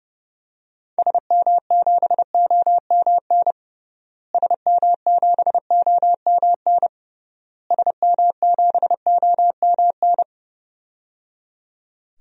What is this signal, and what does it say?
Morse code, heard as a keyed tone, about 30 wpm, 700 Hz. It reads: SM7OMN SM7OMN SM7OMN
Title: DXCC-100 Call Signs - 6 Characters - Repeated 3x - 30wpm